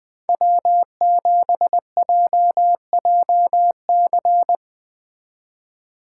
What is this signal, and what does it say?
Morse code, heard as a keyed tone, about 21 wpm, 700 Hz. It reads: W7JJC